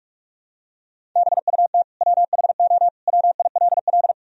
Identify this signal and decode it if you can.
Morse code, heard as a keyed tone, about 41 wpm, 700 Hz. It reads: BUT WHO WILL
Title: Sentences from Top 1000 Words - Review Part 1 - 45wpm